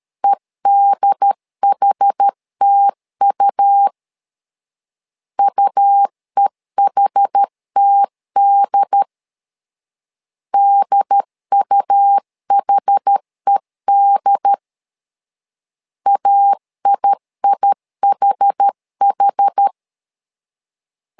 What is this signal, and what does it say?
Morse code, heard as a keyed tone, about 13 wpm, 800 Hz. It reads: EDHTU UEHTD DUHED AIIHH
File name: LEZIONI / 2 UHDN / 2 UHD 12wpm 1540